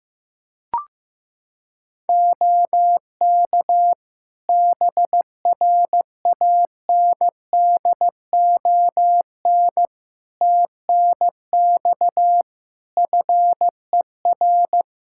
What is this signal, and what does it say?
Morse code, heard as a keyed tone, about 15 wpm, 700 Hz. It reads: OK BRANDON TNX FER
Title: K1USN SST QSO Practice 15wpm